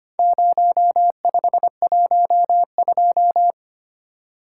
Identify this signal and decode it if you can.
Morse code, heard as a keyed tone, about 25 wpm, 700 Hz. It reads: 0512
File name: practice / 25wpm / Four Letter-Number Combinations - 25wpm